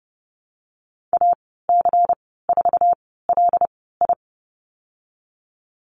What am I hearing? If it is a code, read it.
Morse code, heard as a keyed tone, about 30 wpm, spaced as for 17 wpm, 700 Hz. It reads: AC4LI
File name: Lesson 19g - Letter w - Call Signs 30wpm